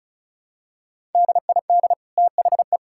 Morse code, heard as a keyed tone, about 36 wpm, 700 Hz. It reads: DID THE
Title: Sentences from Top 1000 Words - Review Part 5 - 35wpm